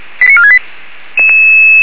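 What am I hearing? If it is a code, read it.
Five-tone selective call (ZVEI), heard as quick runs of five tones, tones of about 70 ms. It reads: 89468
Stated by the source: ZVEI five-tone selective call 89464 received on SDR.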